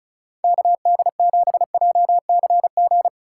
Morse code, heard as a keyed tone, about 34 wpm, 700 Hz. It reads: KD7JCG
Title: USA Call Signs - 2x Character Spacing - 35wpm